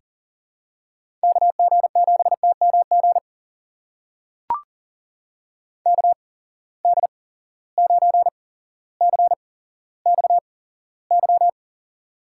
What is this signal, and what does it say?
Morse code, heard as a keyed tone, about 40 wpm, spaced as for 17 wpm, 700 Hz. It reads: KG7TMG KD9CXY
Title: USA Call Signs - 8x Character Spacing - 40wpm